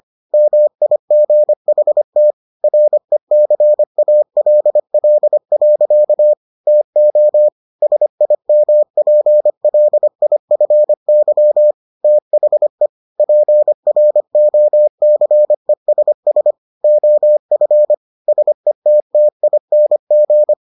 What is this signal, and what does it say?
Morse code, heard as a keyed tone, about 25 wpm, 600 Hz. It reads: MIGHT RECALL. TO SIMPLIFY THE PROCESS OF SETTING